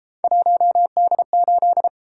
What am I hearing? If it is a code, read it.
Morse code, heard as a keyed tone, about 31 wpm, 700 Hz. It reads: 1D8